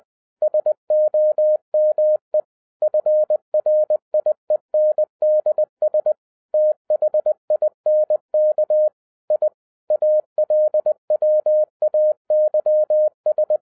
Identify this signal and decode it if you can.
Morse code, heard as a keyed tone, about 20 wpm, 610 Hz. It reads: SOME FRIENDS THINK I ALWAYS
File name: 2011-10-mp3 / n13-TIPS FROM MY 34 YEARS OF EXPERIENCE 20wpm